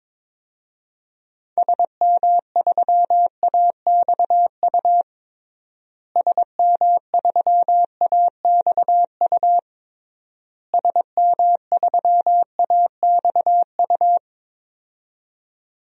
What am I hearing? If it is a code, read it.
Morse code, heard as a keyed tone, about 22 wpm, 700 Hz. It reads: SM3AXU SM3AXU SM3AXU
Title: DXCC-100 Call Signs - 6 Characters - Repeated 3x - 22wpm